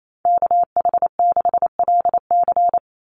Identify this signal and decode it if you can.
Morse code, heard as a keyed tone, about 28 wpm, 700 Hz. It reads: KH6LC
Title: Call Signs Repeated 3x - 28wpm